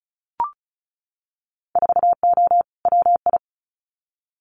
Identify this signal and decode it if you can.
Morse code, heard as a keyed tone, about 36 wpm, 700 Hz. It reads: 4O WI
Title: Winter Field Day Exchanges 35wpm